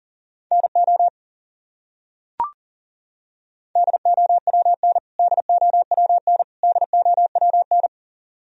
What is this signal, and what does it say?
Morse code, heard as a keyed tone, about 40 wpm, 700 Hz. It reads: NO DOWN DOWN DOWN
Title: Top 100 Words Repeated 40wpm